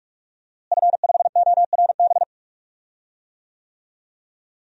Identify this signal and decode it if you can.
Morse code, heard as a keyed tone, about 46 wpm, 700 Hz. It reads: F5ORB
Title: DXCC-100 Call Signs - 5 Characters - Repeated 2x - 45wpm